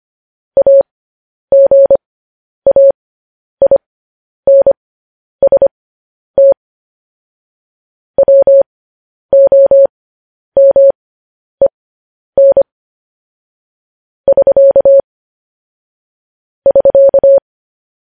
Morse code, heard as a keyed tone, about 25 wpm, spaced as for 10 wpm, 550 Hz. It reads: AGAINST WOMEN <SK> <SK>
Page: news-headlines-in-morse-code-at-10-wpm